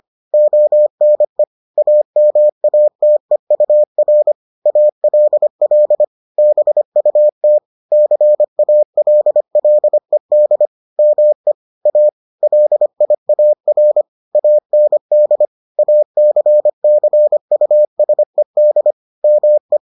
Morse code, heard as a keyed tone, about 25 wpm, 600 Hz. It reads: ONE AMATEUR ALL BUT CALLED ME A LIAR AND ACCUSED ME